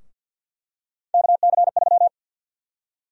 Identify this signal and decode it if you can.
Morse code, heard as a keyed tone, about 53 wpm, 700 Hz. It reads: KC3
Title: USA Call Sign Prefixes - Repeated 2x - 50wpm